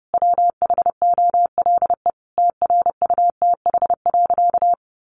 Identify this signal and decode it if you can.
Morse code, heard as a keyed tone, about 30 wpm, 700 Hz. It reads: WHOLE TRUTH.
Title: Sentences from Top 800 Words 30wpm